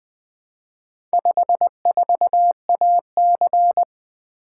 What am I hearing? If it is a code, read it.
Morse code, heard as a keyed tone, about 20 wpm, 700 Hz. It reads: 54AC
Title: Four Letter-Number Combinations - 20wpm